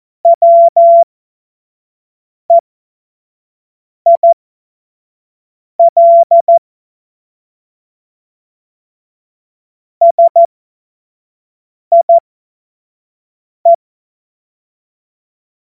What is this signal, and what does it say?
Morse code, heard as a keyed tone, about 14 wpm, spaced as for 5 wpm, 680 Hz. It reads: WEIL SIE
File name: Morsekurs / Witze / Witz 26. April 2022 25 F70